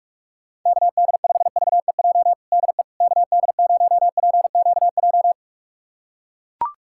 Morse code, heard as a keyed tone, about 42 wpm, 700 Hz. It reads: KD5VEJ DE KD0PQJ
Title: USA Callsign DE Callsign - 6 Characters - 45wpm